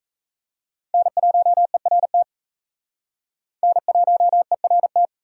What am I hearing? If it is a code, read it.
Morse code, heard as a keyed tone, about 34 wpm, 700 Hz. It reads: N1ERT N1ERT
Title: Speed Racing - USA Call Signs 28wpm